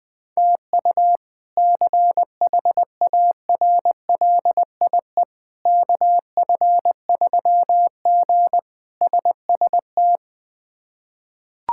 Morse code, heard as a keyed tone, about 20 wpm, 700 Hz. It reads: TU CHARLIE KF3G SST
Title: K1USN SST QSO Practice - 20wpm